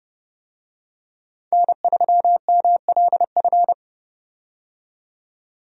Morse code, heard as a keyed tone, about 30 wpm, 700 Hz. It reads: N3MLF
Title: USA Call Signs - 5 Characters - 30wpm